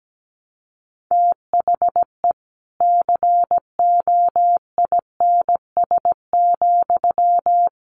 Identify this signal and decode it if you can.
Morse code, heard as a keyed tone, about 17 wpm, 700 Hz. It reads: THE COINS,
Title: A Sound Of Thunder 17wpm